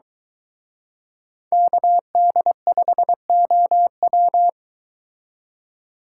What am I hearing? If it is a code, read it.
Morse code, heard as a keyed tone, about 22 wpm, 700 Hz. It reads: KD5OW